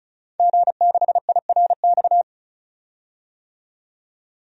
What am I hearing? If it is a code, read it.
Morse code, heard as a keyed tone, about 34 wpm, 700 Hz. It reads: G6IRX
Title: DXCC-100 Call Signs - 5 Characters - 35wpm